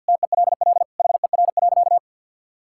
Morse code, heard as a keyed tone, about 50 wpm, 700 Hz. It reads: TELL HER.
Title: Sentences from Top 1000 Words - Review Part 5 - 50wpm